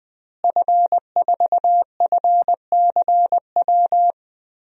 Morse code, heard as a keyed tone, about 20 wpm, 700 Hz. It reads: F4FCW
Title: DXCC-100 Call Signs - 5 Characters - Repeated 3x - 20wpm